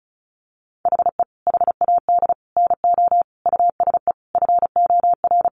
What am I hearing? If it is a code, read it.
Morse code, heard as a keyed tone, about 34 wpm, 700 Hz. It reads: HE HAD NO USE FOR
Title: Sentences from Top 1200 Words 35wpm